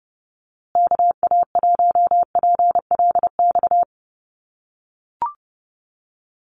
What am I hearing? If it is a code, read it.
Morse code, heard as a keyed tone, about 30 wpm, 700 Hz. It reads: KA1PLX
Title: USA Call Signs Repeated 3x 30wpm